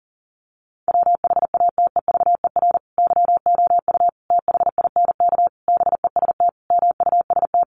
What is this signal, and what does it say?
Morse code, heard as a keyed tone, about 37 wpm, 700 Hz. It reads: WHATEVER YOU THINK BEST MUST
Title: Sentences from Top 1100 Words 40wpm